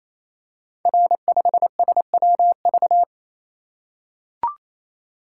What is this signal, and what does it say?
Morse code, heard as a keyed tone, about 29 wpm, 700 Hz. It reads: R5SWV
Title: Code Groups - 5 Letter:Numbers - 28wpm